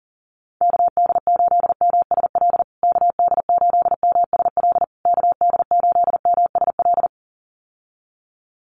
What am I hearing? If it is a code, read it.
Morse code, heard as a keyed tone, about 40 wpm, 700 Hz. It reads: KD8MSL KD8MSL KD8MSL
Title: USA Call Signs Repeated 3x 40wpm